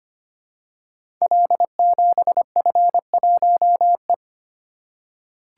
Morse code, heard as a keyed tone, about 25 wpm, 700 Hz. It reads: L7F1E
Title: Code Groups - 5 Letter:Numbers - 25wpm